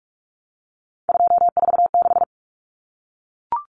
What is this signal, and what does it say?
Morse code, heard as a keyed tone, about 44 wpm, 700 Hz. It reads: 246
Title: Numbers - 3-Digit 45wpm